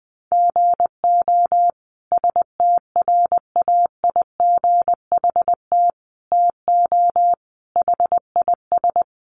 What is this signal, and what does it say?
Morse code, heard as a keyed tone, about 20 wpm, 700 Hz. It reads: GO STRAIGHT TO HIS
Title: Sentences from Top 1000 Words 20wpm